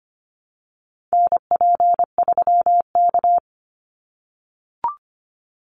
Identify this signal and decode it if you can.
Morse code, heard as a keyed tone, about 25 wpm, 700 Hz. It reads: NP3K
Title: Call Signs - 25wpm